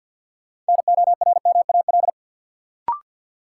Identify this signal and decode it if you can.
Morse code, heard as a keyed tone, about 48 wpm, 700 Hz. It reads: NORMAL
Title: Sentences from Top 1200 Words - New Word Drill - 50wpm